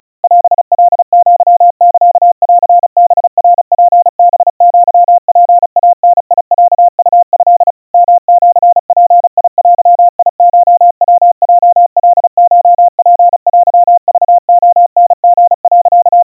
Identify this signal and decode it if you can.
Morse code (Japanese Wabun code), heard as a keyed tone, about 36 wpm, 700 Hz. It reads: オカアサンハナツハアツイダロウト ヨシヅデコヤヲカコツテクレタリ、
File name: kawa / Morse sample / kamyu 130 cpm 700 Hz Crispy